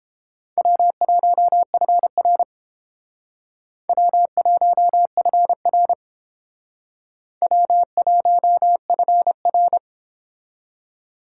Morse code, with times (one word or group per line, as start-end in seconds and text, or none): 0.6-2.4 W1FR
3.9-5.9 W1FR
7.4-9.8 W1FR